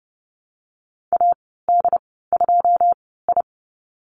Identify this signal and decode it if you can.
Morse code, heard as a keyed tone, about 30 wpm, spaced as for 17 wpm, 700 Hz. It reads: AD2I
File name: Lesson 14g - Number 2 - Call Signs 30wpm